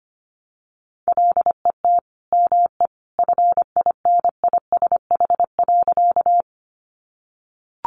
Morse code, heard as a keyed tone, about 25 wpm, 700 Hz. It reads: LET ME FINISH.
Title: Sentences from Top 1200 Words 25wpm